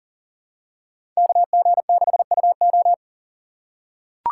Morse code, heard as a keyed tone, about 40 wpm, 700 Hz. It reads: KG6UO
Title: USA Call Signs - 5 Characters - 3x Character Spacing - 40wpm